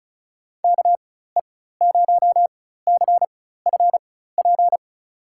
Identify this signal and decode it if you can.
Morse code, heard as a keyed tone, about 34 wpm, spaced as for 16 wpm, 700 Hz. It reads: KE0CFP
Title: USA Call Signs - 4x Character Spacing - 35wpm